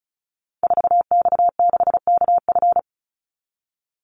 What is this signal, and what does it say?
Morse code, heard as a keyed tone, about 34 wpm, 700 Hz. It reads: 4X6KF
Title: Call Signs Repeated 2x - 35wpm